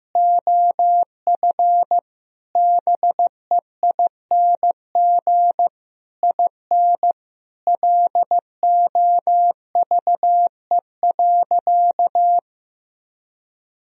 Morse code, heard as a keyed tone, about 15 wpm, 700 Hz. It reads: OF BEING IN LOVE.
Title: Sentences from Top 1000 Words - Review Part 3 - 15wpm